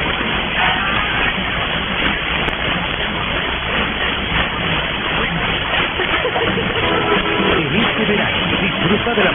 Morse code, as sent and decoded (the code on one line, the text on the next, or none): none